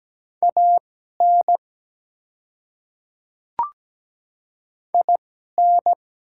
Morse code, heard as a keyed tone, about 17 wpm, spaced as for 12 wpm, 700 Hz. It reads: AN IN
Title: 2x Character Spacing - Top 200 Words - 17wpm